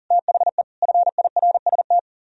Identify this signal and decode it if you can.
Morse code, heard as a keyed tone, about 54 wpm, spaced as for 47 wpm, 700 Hz. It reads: THE FIRST
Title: Speed-Racing - Sets of 3 Words 40wpm